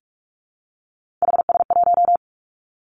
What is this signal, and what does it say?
Morse code, heard as a keyed tone, about 46 wpm, 700 Hz. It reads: HS1
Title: Three Challenging Letter-Number Combinations 45wpm